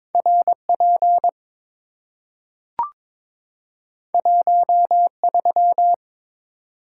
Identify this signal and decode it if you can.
Morse code, heard as a keyed tone, about 22 wpm, 700 Hz. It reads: RP 13
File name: practice / 22wpm / Two Letter-Number Combinations - 22wpm